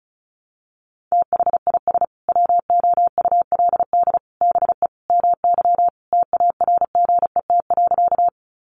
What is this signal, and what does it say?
Morse code, heard as a keyed tone, about 34 wpm, 700 Hz. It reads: THIS WOULD BE MY TARGET.
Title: Sentences from Top 1200 Words 35wpm